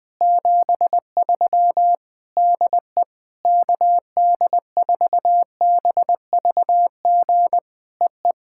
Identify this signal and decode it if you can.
Morse code, heard as a keyed tone, about 20 wpm, 700 Hz. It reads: 73 DE KD4BVG EE
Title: POTA QSO Practice - 20wpm